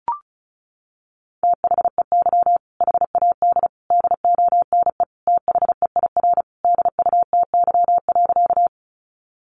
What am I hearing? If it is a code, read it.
Morse code, heard as a keyed tone, about 34 wpm, 700 Hz. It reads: THEY HAD DONE THEIR DUTY.
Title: Sentences from Top 1500 Words 35wpm